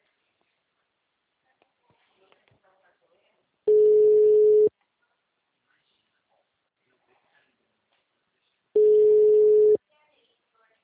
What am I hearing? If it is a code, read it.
Ringback tone (Europe, 425 Hz), with speech.
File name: out- -2006- - - .118